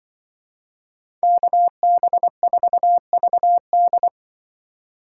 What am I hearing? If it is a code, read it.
Morse code, heard as a keyed tone, about 24 wpm, 700 Hz. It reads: KB4VD
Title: Speed Racing - USA 2x2 Call Signs 20wpm